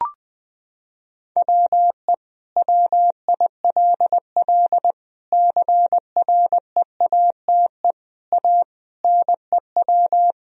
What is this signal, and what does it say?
Morse code, heard as a keyed tone, about 21 wpm, 700 Hz. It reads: WE WILL CREATE A NEW
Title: Sentences from Top 1000 Words - Review Part 3 - 20wpm